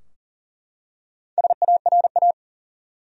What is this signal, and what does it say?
Morse code, heard as a keyed tone, about 40 wpm, 700 Hz. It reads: SARA
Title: USA Common Names - Max 5 Letters - 40wpm